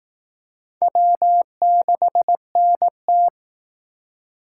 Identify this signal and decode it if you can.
Morse code, heard as a keyed tone, about 18 wpm, 700 Hz. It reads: W6NT
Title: Speed Racing - USA 1x2 Call Signs 15wpm